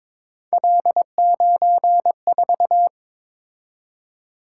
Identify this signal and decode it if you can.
Morse code, heard as a keyed tone, about 22 wpm, 700 Hz. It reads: L94
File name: practice / 22wpm / Three Letter-Number Combinations - 22wpm